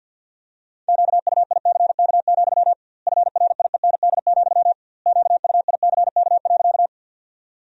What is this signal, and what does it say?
Morse code, heard as a keyed tone, about 50 wpm, 700 Hz. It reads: QUICK, FRIEND, QUICK.
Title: 2x Word-Spacing - Sentences from Top 500 Words - 50wpm